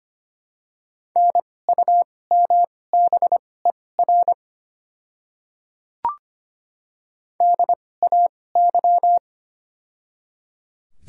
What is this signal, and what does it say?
Morse code, heard as a keyed tone, about 25 wpm, spaced as for 18 wpm, 700 Hz. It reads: NUMBER DAY
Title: 2x Character Spacing - Top 100 Words - 25wpm